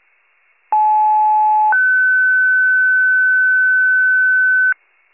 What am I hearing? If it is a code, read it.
Two-tone page, heard as a one-second tone, then a three-second tone, about 860 then 1500 Hz. It is heard from 0.7 to 4.7 s.